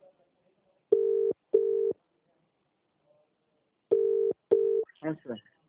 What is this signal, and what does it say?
Ringback tone (UK), 400 + 450 Hz, with speech.